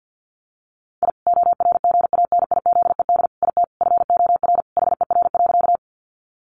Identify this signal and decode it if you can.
Morse code, heard as a keyed tone, about 48 wpm, 700 Hz. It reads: I ORGANIZED IT FOR HER.
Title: Sentences from Top 1300 Words 50wpm